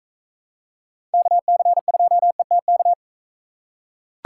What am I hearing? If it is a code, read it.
Morse code, heard as a keyed tone, about 40 wpm, 700 Hz. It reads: KC2ETK